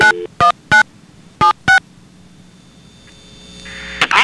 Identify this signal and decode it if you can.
DTMF, heard as telephone touch tones, tones of about 120 ms. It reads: 919 76